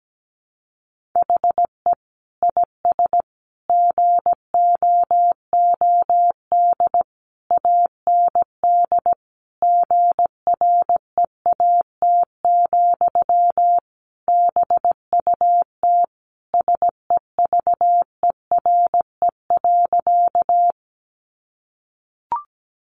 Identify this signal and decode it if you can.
Morse code, heard as a keyed tone, about 17 wpm, 700 Hz. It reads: HE IS GOOD AND GREAT, BUT SEVERE.
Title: Sentences from Top 1400 Words 17wpm